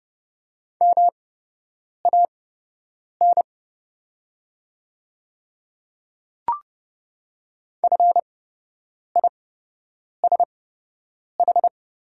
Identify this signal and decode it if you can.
Morse code, heard as a keyed tone, about 30 wpm, spaced as for 8 wpm, 700 Hz. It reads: MAN FISH
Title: 8x Character Spacing - Top 300 Words - 30wpm